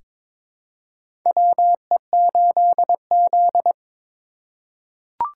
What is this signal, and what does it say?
Morse code, heard as a keyed tone, about 22 wpm, 700 Hz. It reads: WE8Z
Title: USA Call Signs - 8x Character Spacing - 22wpm